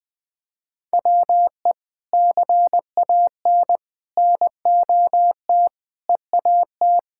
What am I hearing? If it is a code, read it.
Morse code, heard as a keyed tone, about 19 wpm, 700 Hz. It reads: WE CAN NOT EAT
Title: Sentences from Top 1000 Words - Review Part 6 - 20wpm